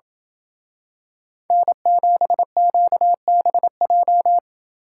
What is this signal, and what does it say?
Morse code, heard as a keyed tone, about 27 wpm, 700 Hz. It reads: N7QBJ